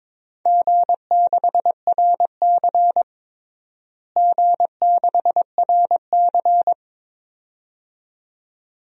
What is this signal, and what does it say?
Morse code, heard as a keyed tone, about 22 wpm, 700 Hz. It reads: G6RC G6RC
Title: DXCC-100 Call Signs - 4 Characters - Repeated 2x - 22wpm